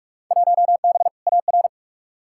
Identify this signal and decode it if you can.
Morse code, heard as a keyed tone, about 45 wpm, 700 Hz. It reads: 1B AR
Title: Field Day Exchanges Encore - 45wpm